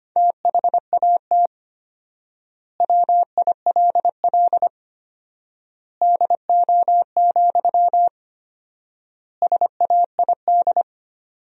Morse code, heard as a keyed tone, about 25 wpm, 700 Hz. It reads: THAT WILL DO, SAID
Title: 4x Word-Spacing - Sentences from Top 100 Words - 25wpm